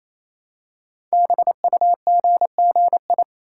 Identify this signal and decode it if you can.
Morse code, heard as a keyed tone, about 27 wpm, 700 Hz. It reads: BUGGI
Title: Code Groups - 5 Letter:Numbers - 28wpm